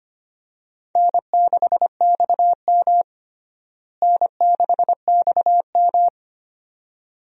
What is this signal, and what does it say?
Morse code, heard as a keyed tone, about 25 wpm, 700 Hz. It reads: N6XM N6XM
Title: USA Call Signs - 4 Characters - Repeated 2x - 25wpm